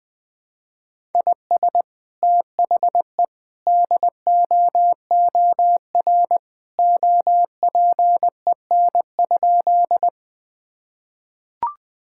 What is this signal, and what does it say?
Morse code, heard as a keyed tone, about 20 wpm, 700 Hz. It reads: IS THE DOOR OPEN?
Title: Sentences with 2-Word Phrases - Spoken Beginning - 20wpm